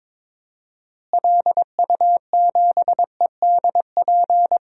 Morse code, heard as a keyed tone, about 22 wpm, 700 Hz. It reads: LU7EDP
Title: DXCC-100 Call Signs - 6 Characters - Repeated 2x - 22wpm